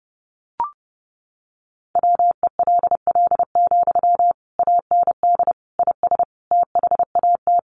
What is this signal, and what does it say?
Morse code, heard as a keyed tone, about 30 wpm, 700 Hz. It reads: WELL, AND IS THAT
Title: Sentences from Top 200 Words 30wpm